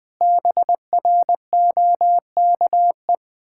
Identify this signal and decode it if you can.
Morse code, heard as a keyed tone, about 20 wpm, 700 Hz. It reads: BROKE
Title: Pangram Challenge v2 - 20wpm